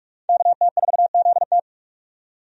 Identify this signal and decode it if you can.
Morse code, heard as a keyed tone, about 44 wpm, 700 Hz. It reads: KT4ZT
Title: USA Call Signs - 5 Characters - 45wpm